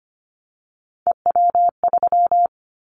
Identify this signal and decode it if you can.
Morse code, heard as a keyed tone, about 25 wpm, 700 Hz. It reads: EW3
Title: Callsign Prefixes for Top 50 DXCC - 25wpm